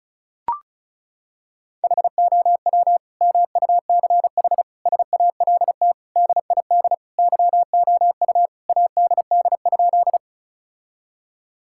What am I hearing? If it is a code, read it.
Morse code, heard as a keyed tone, about 34 wpm, 700 Hz. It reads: HOW MUCH SALT DID YOU ADD?
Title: Sentences from Top 1000 Words - Review Part 5 - 35wpm